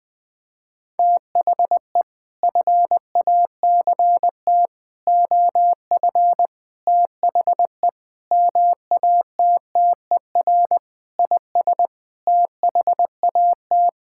Morse code, heard as a keyed tone, about 20 wpm, 700 Hz. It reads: THE FACT OF THE MATTER IS THAT
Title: Sentences with 8-Word Phrases - 20wpm